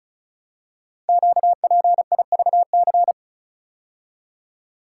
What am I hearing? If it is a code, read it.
Morse code, heard as a keyed tone, about 36 wpm, 700 Hz. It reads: QPIVC
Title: Code Groups - 5 Letters - 35wpm